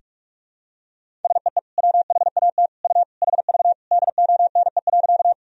Morse code, heard as a keyed tone, about 47 wpm, 700 Hz. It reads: SEE WHAT U HV DONE.
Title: Sentences from Top 500 Words and Abbreviations 45wpm